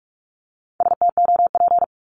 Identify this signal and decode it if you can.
Morse code, heard as a keyed tone, about 46 wpm, 700 Hz. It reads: STOP
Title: Multi-letter Phonemes 45wpm